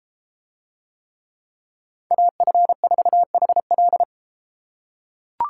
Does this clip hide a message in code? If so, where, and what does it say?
Yes: Morse code from 2.1 to 4.0 s, heard as a keyed tone, about 33 wpm, 700 Hz. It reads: AF4HL